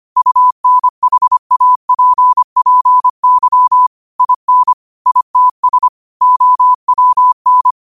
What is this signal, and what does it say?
Morse code, heard as a keyed tone, about 25 wpm, 1000 Hz. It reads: UNHAPPY IN ITS OWN
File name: qotdfiles / 25wpm